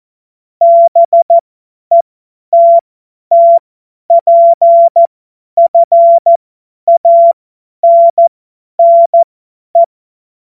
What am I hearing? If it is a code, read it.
Morse code, heard as a keyed tone, about 14 wpm, spaced as for 10 wpm, 680 Hz. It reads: BETTPFANNE
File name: Morsekurs / Witze / Witz 04. Juli 2023 50 F70